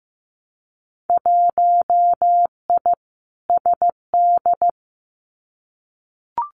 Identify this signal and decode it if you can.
Morse code, heard as a keyed tone, about 15 wpm, 700 Hz. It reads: 1I SD
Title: Winter Field Day Exchanges 15wpm